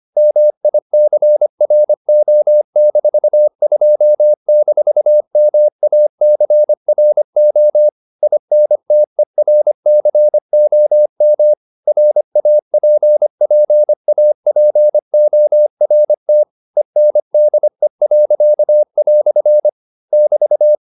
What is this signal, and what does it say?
Morse code, heard as a keyed tone, about 25 wpm, 600 Hz. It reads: MICRO-2-MACRO INTERCOM RAPPAPORT ENDE." <BT>